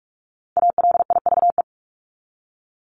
Morse code, heard as a keyed tone, about 44 wpm, 700 Hz. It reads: ALIVE
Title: Top 1751-2000 Words 45wpm